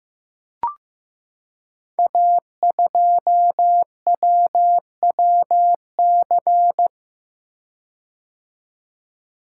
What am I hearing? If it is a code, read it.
Morse code, heard as a keyed tone, about 15 wpm, 700 Hz. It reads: A2WWC
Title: Code Groups - 5 Letter:Numbers - 15wpm